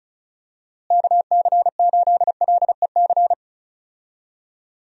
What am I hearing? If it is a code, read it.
Morse code, heard as a keyed tone, about 34 wpm, 700 Hz. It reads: KC8LEC
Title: USA Call Signs - 6 Characters - Repeated 3x - 35wpm